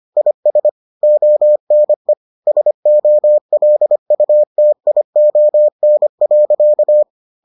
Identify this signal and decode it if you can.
Morse code, heard as a keyed tone, about 25 wpm, 600 Hz. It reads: IS ONE SOLUTION.